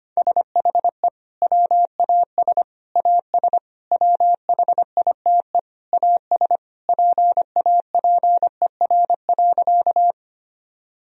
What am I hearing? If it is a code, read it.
Morse code, heard as a keyed tone, about 25 wpm, 700 Hz. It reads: SHE WAS AS WHITE AS PAPER.
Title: Sentences from Top 1000 Words - Review Part 1 - 25wpm